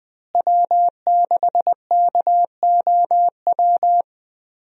Morse code, heard as a keyed tone, about 20 wpm, 700 Hz. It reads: W6KOW
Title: USA Call Signs - 5 Characters - Repeated 2x - 20wpm